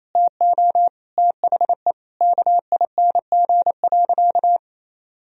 Morse code, heard as a keyed tone, about 27 wpm, 700 Hz. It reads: TO THE KING.
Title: Sentences from Top 1000 Words - Review Part 4 - 28wpm